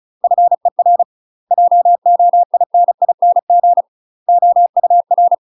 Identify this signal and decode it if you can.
Morse code, heard as a keyed tone, about 36 wpm, 700 Hz. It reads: FER JOINING OUR
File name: UK MASTER QSOs TRAINING / 35WPM / UK MASTER QSOs 0009